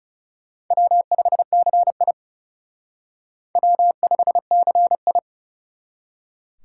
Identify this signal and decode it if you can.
Morse code, heard as a keyed tone, about 40 wpm, 700 Hz. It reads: W5CI W5CI